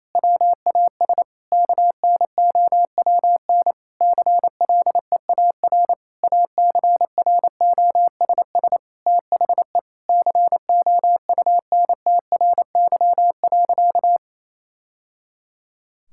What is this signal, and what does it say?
Morse code, heard as a keyed tone, about 28 wpm, 700 Hz. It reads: WAS KNOWN CLEAR ACROSS THE COUNTRY.
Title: Sentences from Top 1000 Words - Review Part 5 - 28wpm